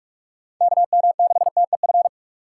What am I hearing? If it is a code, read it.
Morse code, heard as a keyed tone, about 46 wpm, 690 Hz. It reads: KM6TEF
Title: USA Call Signs - 3x Character Spacing - 45wpm